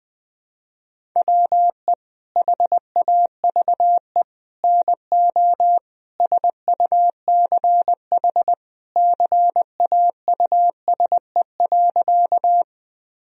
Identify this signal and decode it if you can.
Morse code, heard as a keyed tone, about 20 wpm, 700 Hz. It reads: WE HAVE NO SUCH CAUSE.
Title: Sentences from Top 1000 Words - Review Part 1 - 20wpm